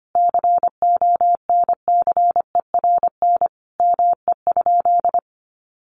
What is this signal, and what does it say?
Morse code, heard as a keyed tone, about 25 wpm, 700 Hz. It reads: CONCERN ME?
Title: Sentences from Top 1300 Words 25wpm